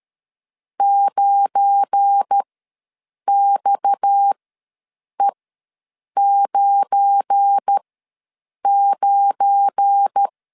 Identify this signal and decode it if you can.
Morse code, heard as a keyed tone, about 13 wpm, spaced as for 7 wpm, 800 Hz. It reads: 9XE99
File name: LEZIONI / 7 GF9Z / 7 GF9 ws 12wpm 3344